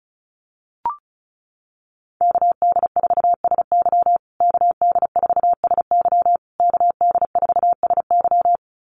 Morse code, heard as a keyed tone, about 34 wpm, 700 Hz. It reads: KD4SY KD4SY KD4SY
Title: USA Call Signs Repeated 3x 35wpm